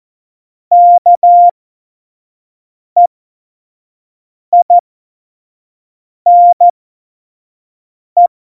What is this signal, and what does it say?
Morse code, heard as a keyed tone, about 14 wpm, spaced as for 5 wpm, 700 Hz. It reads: KEINE
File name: Morsekurs / Witze / Witz 04. Dezember 2018, 25 F70